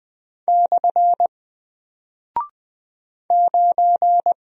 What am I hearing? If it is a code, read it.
Morse code, heard as a keyed tone, about 20 wpm, 700 Hz. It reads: / 9
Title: Single Character (Letter:Number:Punctuation:Prosign) Alt 20wpm